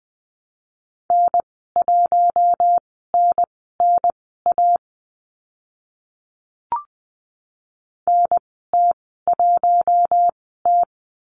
Morse code, heard as a keyed tone, about 20 wpm, spaced as for 14 wpm, 700 Hz. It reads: N1NNA NT1T
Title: 048 Lesson 9g - Number 4 - Call Signs 20wpm